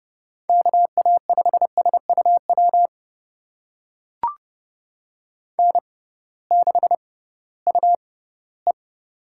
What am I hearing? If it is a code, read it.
Morse code, heard as a keyed tone, about 32 wpm, spaced as for 18 wpm, 700 Hz. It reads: KA5SUW N6UE
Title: USA Call Signs - 6x Character Spacing - 30wpm